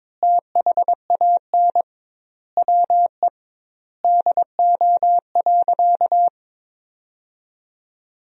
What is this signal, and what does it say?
Morse code, heard as a keyed tone, about 22 wpm, 700 Hz. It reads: THAN WE DO.
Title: 2x Word-Spacing - Sentences from Top 100 Words - 22wpm